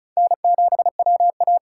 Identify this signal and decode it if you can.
Morse code, heard as a keyed tone, about 47 wpm, 700 Hz. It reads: N7WA